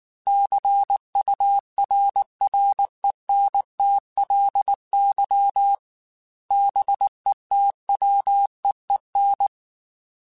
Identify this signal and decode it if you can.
Morse code, heard as a keyed tone, about 19 wpm, 800 Hz. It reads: CURRENTLY BETWEEN
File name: Weather Sets / Weather Sets-mp3s / Weather 5 22 @18